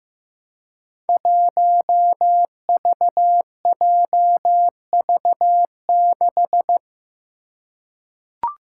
Morse code, heard as a keyed tone, about 15 wpm, 700 Hz. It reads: 1VJV6
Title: Five Challenging Letter-Number Combinations 15wpm